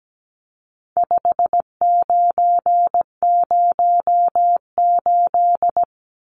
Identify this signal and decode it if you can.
Morse code, heard as a keyed tone, about 17 wpm, 700 Hz. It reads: 5908
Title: Numbers - 4-Digit 17wpm